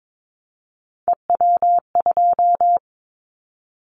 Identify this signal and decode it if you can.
Morse code, heard as a keyed tone, about 22 wpm, 700 Hz. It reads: EW2
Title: Callsign Prefixes for Top 100 DXCC - 22wpm